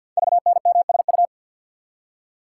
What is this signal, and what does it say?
Morse code, heard as a keyed tone, about 48 wpm, 700 Hz. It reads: VNMSU
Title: Code Groups - 5 Letters - 50wpm